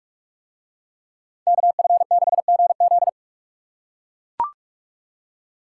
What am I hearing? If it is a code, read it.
Morse code, heard as a keyed tone, about 39 wpm, 700 Hz. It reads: KF6GZ